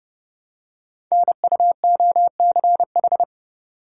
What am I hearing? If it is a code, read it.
Morse code, heard as a keyed tone, about 28 wpm, 700 Hz. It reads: NUOCH
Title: Code Groups - 5 Letters - 30wpm